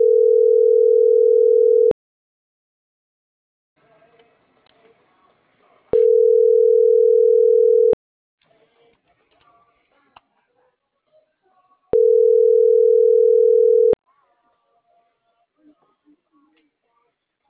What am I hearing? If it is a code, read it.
Ringback tone (North America), 440 + 480 Hz.